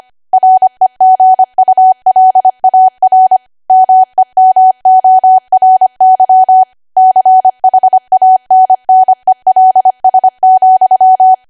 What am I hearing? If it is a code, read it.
Morse code, heard as a keyed tone, about 25 wpm, 750 Hz. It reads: REGULAR MEMORY CHANNELS,
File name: Archive / 25 WPM / 25WPM